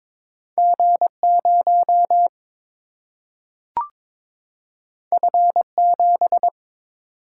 Morse code, heard as a keyed tone, about 22 wpm, 700 Hz. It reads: G 0 F 7
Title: Single-Letter Single-Number 22wpm